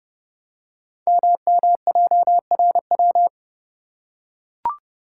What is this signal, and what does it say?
Morse code, heard as a keyed tone, about 30 wpm, 700 Hz. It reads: MMJRW
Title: Code Groups - 5 Letters - 30wpm